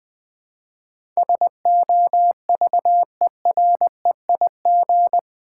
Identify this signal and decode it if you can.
Morse code, heard as a keyed tone, about 20 wpm, 700 Hz. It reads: SOVEREIG
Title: Build A Word - 4 Letters At A Time - 11-Letter Words - 20wpm